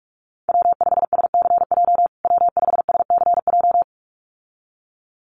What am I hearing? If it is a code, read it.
Morse code, heard as a keyed tone, about 44 wpm, 700 Hz. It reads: W5SCJ W5SCJ
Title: USA Call Signs Repeated 2x 45wpm